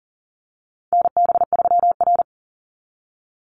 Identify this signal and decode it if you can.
Morse code, heard as a keyed tone, about 40 wpm, 700 Hz. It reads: NB3R
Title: Call Signs - 40wpm